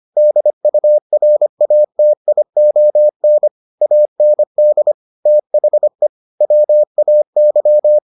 Morse code, heard as a keyed tone, about 25 wpm, 600 Hz. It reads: DURATION AND THE WAY